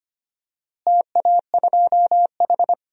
Morse code, heard as a keyed tone, about 25 wpm, 700 Hz. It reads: TA2H
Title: DXCC-100 Call Signs - 4 Characters - Repeated 2x - 25wpm